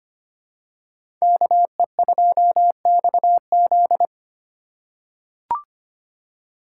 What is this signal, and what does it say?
Morse code, heard as a keyed tone, about 25 wpm, 700 Hz. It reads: KE2XZ
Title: USA Call Signs - 5 Characters - 3x Character Spacing - 25wpm